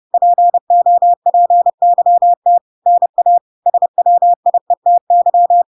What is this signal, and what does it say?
Morse code, heard as a keyed tone, about 30 wpm, 700 Hz. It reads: POPYT NA SWIETY